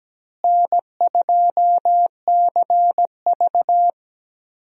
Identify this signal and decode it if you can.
Morse code, heard as a keyed tone, about 17 wpm, 700 Hz. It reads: N2CV
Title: USA Call Signs - 4 Characters - Repeated 2x - 17wpm